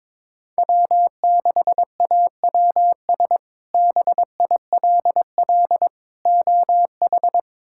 Morse code, heard as a keyed tone, about 22 wpm, 700 Hz. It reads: W6AWS BILL OH
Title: K1USN SST QSO Practice - 22wpm